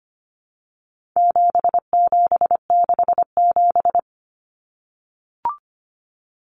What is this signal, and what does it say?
Morse code, heard as a keyed tone, about 25 wpm, 700 Hz. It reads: 7767
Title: Numbers - 4-Digit 25wpm